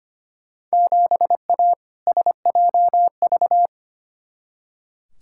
Morse code, heard as a keyed tone, about 25 wpm, 700 Hz. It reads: 7A SJV
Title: Field Day Exchanges Encore - 25wpm